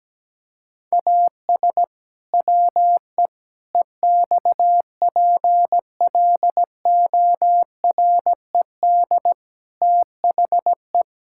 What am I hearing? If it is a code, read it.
Morse code, heard as a keyed tone, about 17 wpm, 700 Hz. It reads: AS WE EXPLORED THE
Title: Pangram Challenge v2 - 17wpm